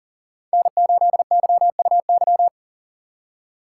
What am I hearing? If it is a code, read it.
Morse code, heard as a keyed tone, about 39 wpm, 700 Hz. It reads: N8YUY
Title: USA Call Signs - 5 Characters - 6x Character Spacing - 40wpm